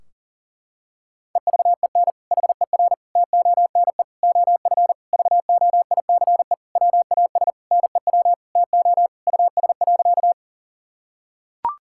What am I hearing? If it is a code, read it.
Morse code, heard as a keyed tone, about 37 wpm, 700 Hz. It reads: EVEN HER TONE OF VOICE WAS NEW TO US.
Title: Sentences from Top 1600 Words 40wpm